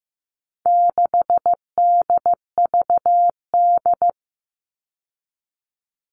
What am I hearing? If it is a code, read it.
Morse code, heard as a keyed tone, about 15 wpm, 700 Hz. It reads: 6DVD
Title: All Cut Numbers - 4-Digit 15wpm